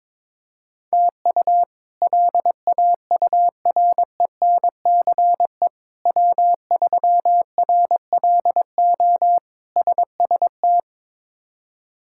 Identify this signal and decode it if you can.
Morse code, heard as a keyed tone, about 22 wpm, 700 Hz. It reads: TU LAURENCE W3RLO SST
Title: K1USN SST QSO Practice - 22wpm